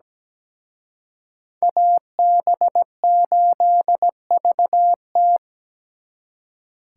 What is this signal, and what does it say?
Morse code, heard as a keyed tone, about 20 wpm, 700 Hz. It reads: AB8VT
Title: Speed Racing - USA 2x2 Call Signs 17wpm